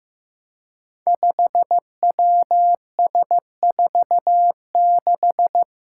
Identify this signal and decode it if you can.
Morse code, heard as a keyed tone, about 15 wpm, 700 Hz. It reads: 5WS46
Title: Five Challenging Letter-Number Combinations 15wpm